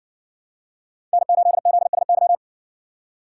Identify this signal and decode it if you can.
Morse code, heard as a keyed tone, about 58 wpm, 700 Hz. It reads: N8ZSY